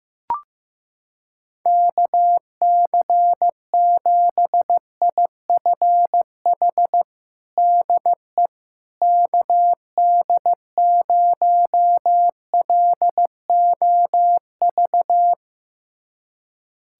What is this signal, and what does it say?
Morse code, heard as a keyed tone, about 15 wpm, 700 Hz. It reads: KC7IFH DE KD0LOV
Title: USA Callsign DE Callsign 15wpm